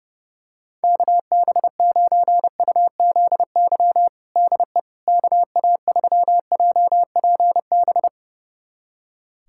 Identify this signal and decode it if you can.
Morse code, heard as a keyed tone, about 30 wpm, 700 Hz. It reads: KB9UZY DE KA3JPB
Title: USA Callsign DE Callsign - 6 Characters - 30wpm